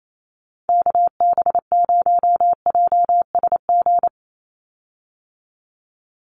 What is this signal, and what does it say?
Morse code, heard as a keyed tone, about 28 wpm, 700 Hz. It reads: KB0JSG
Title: USA Call Signs 28wpm